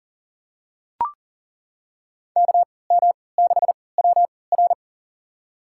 Morse code, heard as a keyed tone, about 40 wpm, spaced as for 23 wpm, 700 Hz. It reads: KM6WR
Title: USA Call Signs - 5 Characters - 3x Character Spacing - 40wpm